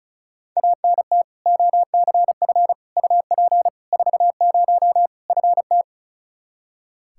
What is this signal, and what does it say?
Morse code, heard as a keyed tone, about 36 wpm, 700 Hz. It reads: ANT OCF UP 40 FT
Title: K1USN SST QSO Practice 35wpm